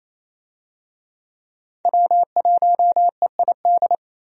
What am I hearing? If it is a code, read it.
Morse code, heard as a keyed tone, about 28 wpm, 700 Hz. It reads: W1EID
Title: USA Call Signs - 5 Characters - 28wpm